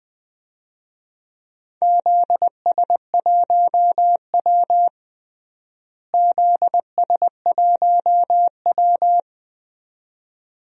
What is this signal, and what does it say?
Morse code, heard as a keyed tone, about 20 wpm, 700 Hz. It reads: ZS1W ZS1W
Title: DXCC-100 Call Signs - 4 Characters - Repeated 2x - 20wpm